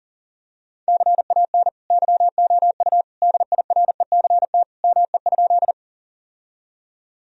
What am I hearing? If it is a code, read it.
Morse code, heard as a keyed tone, about 40 wpm, 700 Hz. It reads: CAN YOU DIRECT ME?
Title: Sentences from Top 1000 Words - Review Part 2 - 40wpm